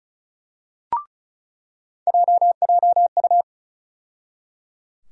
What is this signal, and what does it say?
Morse code, heard as a keyed tone, about 36 wpm, 700 Hz. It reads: JJU
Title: USA Call Sign Postfixes - 3 Letters 35wpm